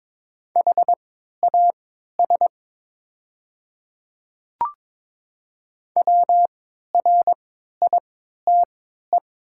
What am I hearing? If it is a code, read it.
Morse code, heard as a keyed tone, about 21 wpm, spaced as for 12 wpm, 700 Hz. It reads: HAS WRITE
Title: 3x Character Spacing - Top 500 Words - 22wpm